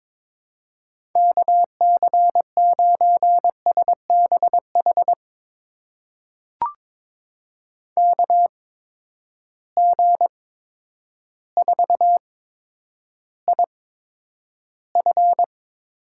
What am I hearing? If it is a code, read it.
Morse code, heard as a keyed tone, about 22 wpm, spaced as for 10 wpm, 700 Hz. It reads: KC9SBH KG4IF
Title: USA Call Signs - 8x Character Spacing - 22wpm